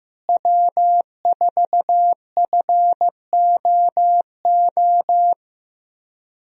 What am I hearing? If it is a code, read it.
Morse code, heard as a keyed tone, about 15 wpm, 700 Hz. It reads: W4FOO
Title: USA Call Signs - 5 Characters - Repeated 3x - 15wpm